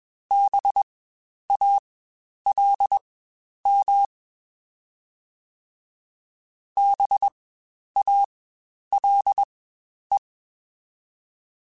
Morse code, heard as a keyed tone, about 21 wpm, spaced as for 10 wpm, 800 Hz. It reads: BALM BALE
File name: Spaced Alphas / Spaced Alphas mp3s / Bs sp @20